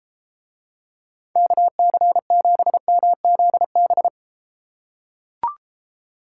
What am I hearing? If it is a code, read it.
Morse code, heard as a keyed tone, about 34 wpm, 700 Hz. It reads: KC7MZB